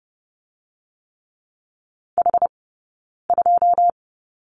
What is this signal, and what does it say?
Morse code, heard as a keyed tone, about 30 wpm, 700 Hz. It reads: H 2
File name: Lesson 20d - Number 3 - Two Characters 30wpm